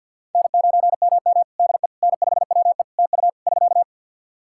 Morse code, heard as a keyed tone, about 50 wpm, 700 Hz. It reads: N9MM DE N5WE TU <SK>
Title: Practice QSO - Exchange 4 - Ending - 50wpm